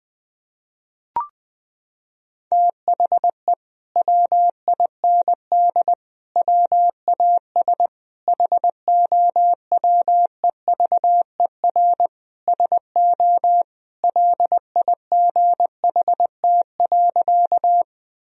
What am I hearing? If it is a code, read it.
Morse code, heard as a keyed tone, about 20 wpm, 700 Hz. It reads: THE WIND WAS HOWEVER SO LIGHT.
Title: Sentences from Top 1000 Words - Review Part 2 - 20wpm